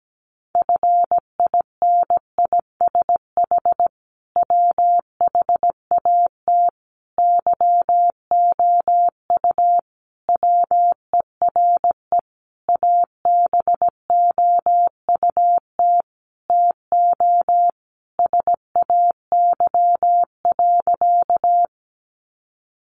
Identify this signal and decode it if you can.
Morse code, heard as a keyed tone, about 17 wpm, 700 Hz. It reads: FINISH WHAT YOU WERE ABOUT TO SAY.
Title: Sentences from Top 1200 Words 17wpm